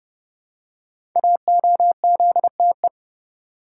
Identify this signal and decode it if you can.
Morse code, heard as a keyed tone, about 29 wpm, 700 Hz. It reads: AOZTE
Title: Code Groups - 5 Letters - 30wpm